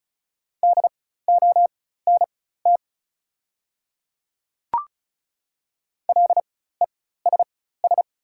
Morse code, heard as a keyed tone, about 36 wpm, spaced as for 16 wpm, 690 Hz. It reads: DONT LESS
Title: 4x Character Spacing - Top 500 Words - 35wpm